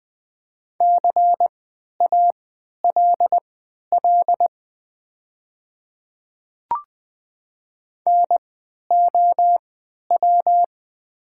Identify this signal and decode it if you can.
Morse code, heard as a keyed tone, about 20 wpm, spaced as for 11 wpm, 700 Hz. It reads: CALL NOW
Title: 3x Character Spacing - Top 100 Words - 20wpm